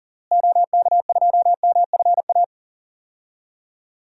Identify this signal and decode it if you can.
Morse code, heard as a keyed tone, about 38 wpm, 700 Hz. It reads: OK2MFA
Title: DXCC-100 Call Signs - 6 Characters - Repeated 3x - 40wpm